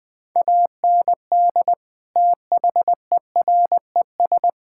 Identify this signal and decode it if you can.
Morse code, heard as a keyed tone, about 20 wpm, 700 Hz. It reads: AND THERES
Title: Sets of 8 Words - 20wpm